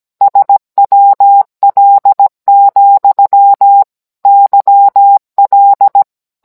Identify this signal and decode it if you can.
Morse code, heard as a keyed tone, about 17 wpm, 800 Hz. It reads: SWL, YL